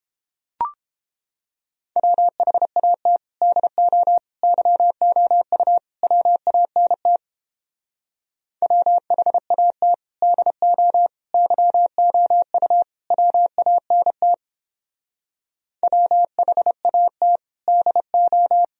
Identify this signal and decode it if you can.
Morse code, heard as a keyed tone, about 29 wpm, 700 Hz. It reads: WHAT DO YOU WANT WHAT DO YOU WANT WHAT DO